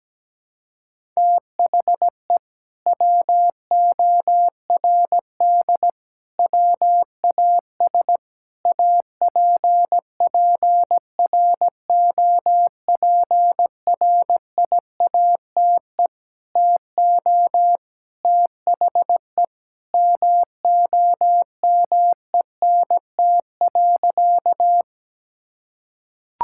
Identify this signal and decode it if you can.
Morse code, heard as a keyed tone, about 17 wpm, 700 Hz. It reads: THE WORD WAS APPROPRIATE TO THE MOMENT.
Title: Sentences from Top 1000 Words - Review Part 5 - 17wpm